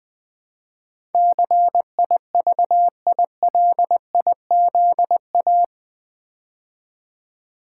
Morse code, heard as a keyed tone, about 20 wpm, 700 Hz. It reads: CIVILIZA
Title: Build A Word - 4 Letters At A Time - 12-Letter Words - 20wpm